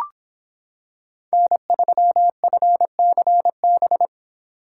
Morse code, heard as a keyed tone, about 26 wpm, 700 Hz. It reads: N3FCB